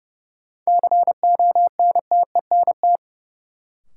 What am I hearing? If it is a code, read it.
Morse code, heard as a keyed tone, about 30 wpm, 700 Hz. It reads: CONTENT
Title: Sentences from Top 900 Words - New Word Drill - 30wpm